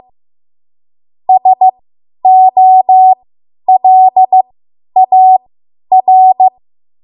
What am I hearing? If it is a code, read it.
Morse code, heard as a keyed tone, about 15 wpm, spaced as for 10 wpm, 750 Hz. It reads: SOLAR
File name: Archive / 10 WPM / 10WPM